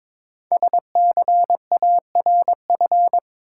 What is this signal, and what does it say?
Morse code, heard as a keyed tone, about 21 wpm, 700 Hz. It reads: SCARF
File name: Difficult Words with Hints - 22wpm